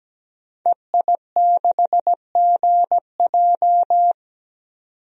Morse code, heard as a keyed tone, about 17 wpm, 700 Hz. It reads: EI6GJ
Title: DXCC-100 Call Signs - 5 Characters - 17wpm